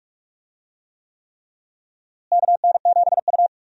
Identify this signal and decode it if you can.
Morse code, heard as a keyed tone, about 46 wpm, 700 Hz. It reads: KN7U
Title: USA Call Signs - 4 Characters - Repeated 3x - 45wpm